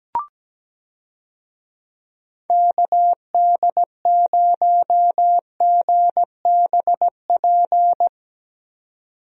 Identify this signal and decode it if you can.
Morse code, heard as a keyed tone, about 17 wpm, 700 Hz. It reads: KD0GBP
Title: USA Call Signs - 6 Characters - 17wpm